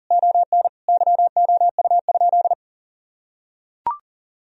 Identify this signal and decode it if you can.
Morse code, heard as a keyed tone, about 40 wpm, 700 Hz. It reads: ON YOU?
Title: Sentences from Top 1000 Words - Review Part 4 - 40wpm